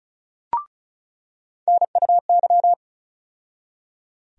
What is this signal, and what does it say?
Morse code, heard as a keyed tone, about 36 wpm, 700 Hz. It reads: NUY
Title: USA Call Sign Postfixes - 3 Letters 35wpm